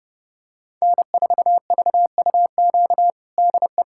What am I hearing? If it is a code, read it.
Morse code, heard as a keyed tone, about 30 wpm, 700 Hz. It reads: N4VUQ DE
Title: USA Callsign DE Callsign 30wpm